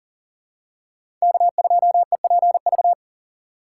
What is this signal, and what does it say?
Morse code, heard as a keyed tone, about 40 wpm, 700 Hz. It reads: K2EPV
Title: USA Call Signs - 5 Characters - 2x Character Spacing - 40wpm